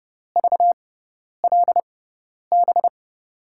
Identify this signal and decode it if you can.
Morse code, heard as a keyed tone, about 32 wpm, spaced as for 10 wpm, 700 Hz. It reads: VLB
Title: USA Call Signs - 6x Character Spacing - 30wpm